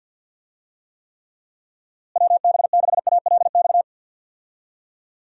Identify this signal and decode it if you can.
Morse code, heard as a keyed tone, about 49 wpm, 700 Hz. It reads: WB6ALX
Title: USA Call Signs - 6 Characters - Repeated 3x - 50wpm